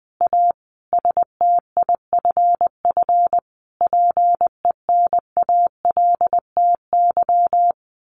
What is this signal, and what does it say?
Morse code, heard as a keyed tone, about 20 wpm, 700 Hz. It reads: A STIFF PENALTY
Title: A Sound Of Thunder 20wpm